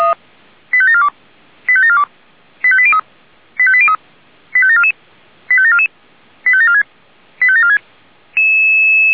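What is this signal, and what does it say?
Five-tone selective call (ZVEI), heard as quick runs of five tones, tones of about 70 ms. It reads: 85742 85742 85803 85803 85744 85744 85746 85746
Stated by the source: Five-tone ZVEI selective call received on an SDR.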